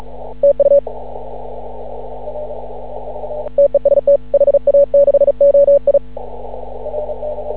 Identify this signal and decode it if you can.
Morse code, heard as a keyed tone, about 36 wpm, 560 Hz. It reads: TU TEST HA6OI
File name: CW / ha6oi 2